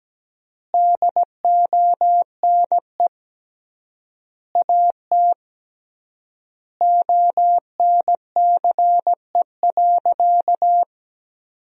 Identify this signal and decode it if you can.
Morse code, heard as a keyed tone, about 17 wpm, 700 Hz. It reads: DONE AT ONCE.
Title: 3x Word-Spacing - Sentences from Top 500 Words - 17wpm